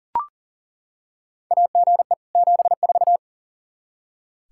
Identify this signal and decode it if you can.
Morse code, heard as a keyed tone, about 38 wpm, 700 Hz. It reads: AGE 74
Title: K1USN SST QSO Practice 40wpm